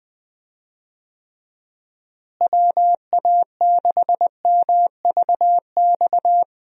Morse code, heard as a keyed tone, about 20 wpm, 700 Hz. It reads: WA6MVX
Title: USA Call Signs - 6 Characters - Repeated 2x - 20wpm